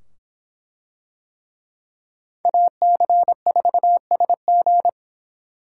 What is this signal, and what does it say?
Morse code, heard as a keyed tone, about 25 wpm, 700 Hz. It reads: AC4SG